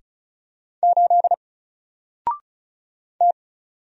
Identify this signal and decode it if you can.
Morse code, heard as a keyed tone, about 36 wpm, 700 Hz. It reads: 8 T
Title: Single Character (Letter:Number:Punctuation:Prosign) Alt 35wpm